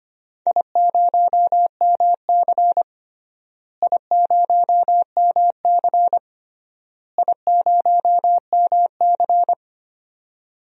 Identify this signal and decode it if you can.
Morse code, heard as a keyed tone, about 25 wpm, 700 Hz. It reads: I0MC I0MC I0MC
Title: DXCC-100 Call Signs - 4 Characters - Repeated 3x - 25wpm